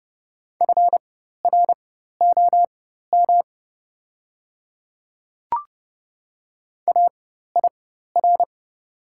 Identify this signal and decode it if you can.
Morse code, heard as a keyed tone, about 30 wpm, spaced as for 14 wpm, 700 Hz. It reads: FROM AIR
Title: 4x Character Spacing - Top 300 Words - 30wpm